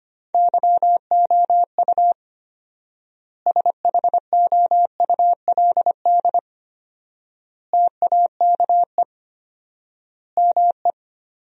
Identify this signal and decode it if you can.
Morse code, heard as a keyed tone, about 25 wpm, 700 Hz. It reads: YOU SHOULD TAKE ME
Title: 4x Word-Spacing - Sentences from Top 300 Words - 25wpm